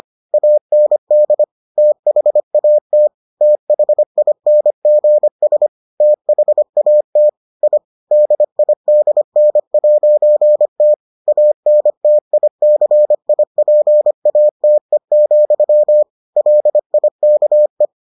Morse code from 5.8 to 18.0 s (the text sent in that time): THAT I DIDN'T ANTICIPATE, LIKE